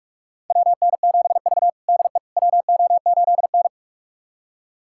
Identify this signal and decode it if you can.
Morse code, heard as a keyed tone, about 46 wpm, 700 Hz. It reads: WN7V DE WO8N
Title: USA Callsign DE Callsign - 4 Characters - 45wpm